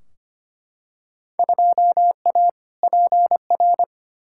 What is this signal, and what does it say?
Morse code, heard as a keyed tone, about 25 wpm, 700 Hz. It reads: 2A PR
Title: Field Day Exchanges Encore - 25wpm